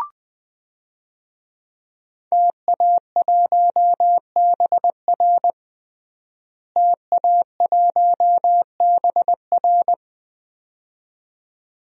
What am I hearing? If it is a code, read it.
Morse code, heard as a keyed tone, about 20 wpm, 700 Hz. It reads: TA1BR TA1BR
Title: DXCC-100 Call Signs - 5 Characters - Repeated 2x - 20wpm